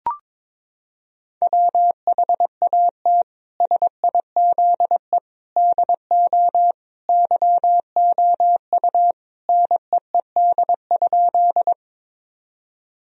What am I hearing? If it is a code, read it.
Morse code, heard as a keyed tone, about 22 wpm, 700 Hz. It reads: WHAT SIZE DO YOU NEED?
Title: Sentences from Top 1000 Words - Review Part 2 - 22wpm